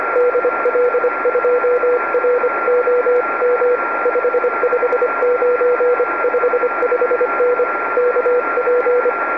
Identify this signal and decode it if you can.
Morse code, heard as a keyed tone, about 25 wpm, 490 Hz, with radio noise. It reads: DL2ROM 559 55N KP
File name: oh1xt-2m